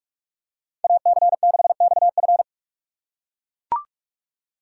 Morse code, heard as a keyed tone, about 43 wpm, 700 Hz. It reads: AC6XF
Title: USA Call Signs - 3x Character Spacing - 45wpm